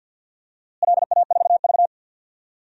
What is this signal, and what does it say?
Morse code, heard as a keyed tone, about 51 wpm, 700 Hz. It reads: LA4V
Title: DXCC-100 Call Signs - 4 Characters - Repeated 2x - 50wpm